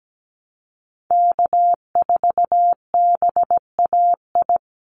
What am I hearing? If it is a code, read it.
Morse code, heard as a keyed tone, about 17 wpm, 700 Hz. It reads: K4BAI
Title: Call Signs Repeated 3x - 17wpm